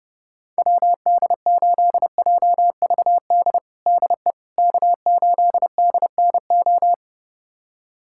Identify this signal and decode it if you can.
Morse code, heard as a keyed tone, about 30 wpm, 700 Hz. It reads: WD8JVD DE K8DNO
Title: USA Callsign DE Callsign 30wpm